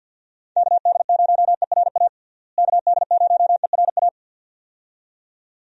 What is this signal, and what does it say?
Morse code, heard as a keyed tone, about 46 wpm, 700 Hz. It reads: KD0ERA KD0ERA
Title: USA Call Signs - 6 Characters - Repeated 2x - 50wpm